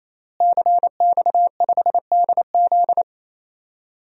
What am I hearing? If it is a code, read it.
Morse code, heard as a keyed tone, about 28 wpm, 700 Hz. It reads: CX5DZ
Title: DXCC-100 Call Signs - 5 Characters - Repeated 3x - 28wpm